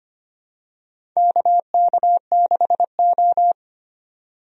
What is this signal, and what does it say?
Morse code, heard as a keyed tone, about 25 wpm, 700 Hz. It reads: KK6O
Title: USA Call Signs - 4 Characters - 25wpm